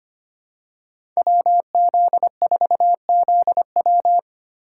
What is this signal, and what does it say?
Morse code, heard as a keyed tone, about 25 wpm, 700 Hz. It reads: WZ4ZW
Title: USA Call Signs - 5 Characters - 2x Character Spacing - 25wpm